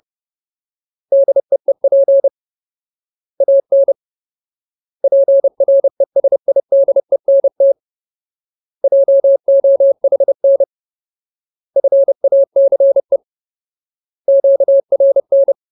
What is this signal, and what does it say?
Morse code, heard as a keyed tone, about 30 wpm, 550 Hz. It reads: DEEP AN PRESIDENT JOHN FACE QRN
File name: Ultimate Word List / 30WPM Ultimate